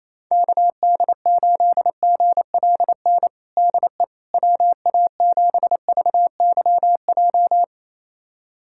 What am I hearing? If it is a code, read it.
Morse code, heard as a keyed tone, about 28 wpm, 700 Hz. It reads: KD8GLN DE WA7VYJ
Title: USA Callsign DE Callsign 28wpm